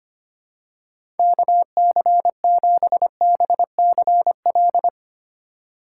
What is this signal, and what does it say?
Morse code, heard as a keyed tone, about 25 wpm, 700 Hz. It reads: KC7BCL
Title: USA Call Signs - 3x Character Spacing - 25wpm